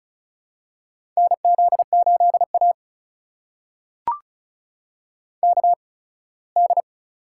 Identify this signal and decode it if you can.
Morse code, heard as a keyed tone, about 36 wpm, spaced as for 21 wpm, 700 Hz. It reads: NZ8A KD
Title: USA Call Signs - 8x Character Spacing - 35wpm